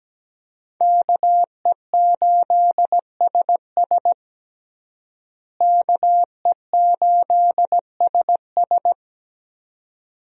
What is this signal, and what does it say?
Morse code, heard as a keyed tone, about 17 wpm, 700 Hz. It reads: KE8SS KE8SS
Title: USA Call Signs - 5 Characters - Repeated 2x - 17wpm